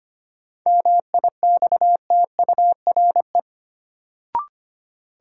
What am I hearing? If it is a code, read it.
Morse code, heard as a keyed tone, about 25 wpm, 700 Hz. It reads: MIXTURE
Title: Sentences from Top 1500 Words - New Word Drill - 25wpm